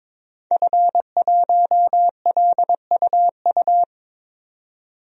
Morse code, heard as a keyed tone, about 22 wpm, 700 Hz. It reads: F1LUU
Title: DXCC-100 Call Signs - 5 Characters - 22wpm